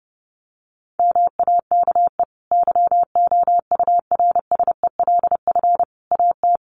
Morse code, heard as a keyed tone, about 30 wpm, 700 Hz. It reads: MAKE YOURSELF AT
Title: Sentences from Top 600 Words 30wpm